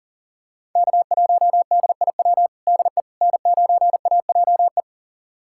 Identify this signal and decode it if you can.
Morse code, heard as a keyed tone, about 40 wpm, 700 Hz. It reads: K1DIW DE N9AJE
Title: USA Callsign DE Callsign - 5 Characters - 40wpm